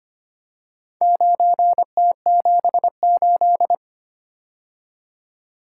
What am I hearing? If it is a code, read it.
Morse code, heard as a keyed tone, about 25 wpm, 700 Hz. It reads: 9T78
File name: practice / 25wpm / Four Letter-Number Combinations - 25wpm